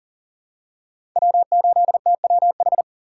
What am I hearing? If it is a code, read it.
Morse code, heard as a keyed tone, about 40 wpm, 690 Hz. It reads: W8TWH
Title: USA Call Signs - 2x Character Spacing - 40wpm